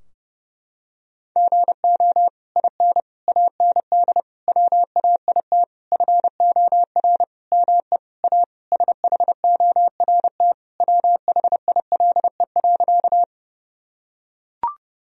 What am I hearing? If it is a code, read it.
Morse code, heard as a keyed tone, about 29 wpm, 700 Hz. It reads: GO IN AND WAIT FOR ME A SHORT WHILE.
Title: Sentences from Top 1000 Words - Review Part 2 - 30wpm